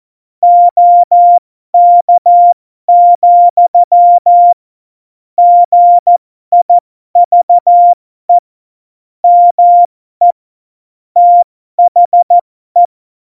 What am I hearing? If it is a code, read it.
Morse code, heard as a keyed tone, about 14 wpm, spaced as for 12 wpm, 700 Hz. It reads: OK, GIVE ME THE
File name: Morsekurs / Witze